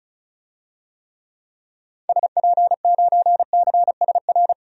Morse code, heard as a keyed tone, about 36 wpm, 700 Hz. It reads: SP9CSR
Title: DXCC-100 Call Signs - 6 Characters - 35wpm